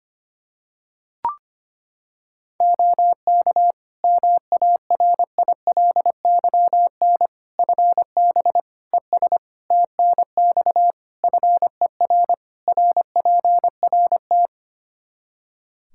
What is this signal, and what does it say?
Morse code, heard as a keyed tone, about 25 wpm, 700 Hz. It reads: OK MARILYN FB ES TNX FER RPRT
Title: Practice QSO - Exchange 2 - Info - 25wpm